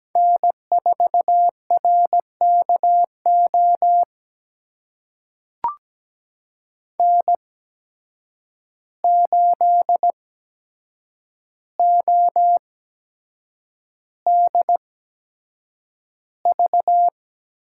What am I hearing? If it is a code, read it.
Morse code, heard as a keyed tone, about 17 wpm, spaced as for 7 wpm, 700 Hz. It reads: N4RKO N8ODV
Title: USA Call Signs - 5 Characters - 8x Character Spacing - 17wpm